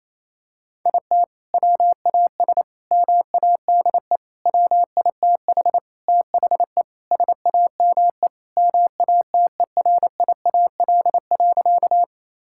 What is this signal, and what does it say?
Morse code, heard as a keyed tone, about 28 wpm, 700 Hz. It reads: IT WAS MADE WITH THE SAME MATERIAL.
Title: Sentences from Top 1000 Words - Review Part 5 - 28wpm